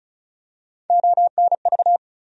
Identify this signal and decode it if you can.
Morse code, heard as a keyed tone, about 34 wpm, 700 Hz. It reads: ONV
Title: USA Call Sign Postfixes - 3 Letters 35wpm